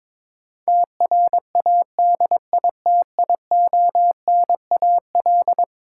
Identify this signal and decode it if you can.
Morse code, heard as a keyed tone, about 22 wpm, 700 Hz. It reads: TRADITIONAL
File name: practice / 22wpm / Eleven Letter Words - 22wpm